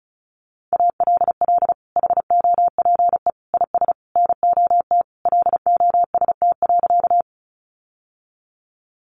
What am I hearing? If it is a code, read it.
Morse code, heard as a keyed tone, about 34 wpm, 700 Hz. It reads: ALL HOPE IS NOT LOST.
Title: Sentences from Top 1400 Words 35wpm